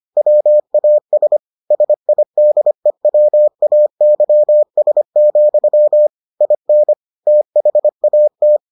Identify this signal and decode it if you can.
Morse code, heard as a keyed tone, about 25 wpm, 600 Hz. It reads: WAS SIDEWAYS, IN THAT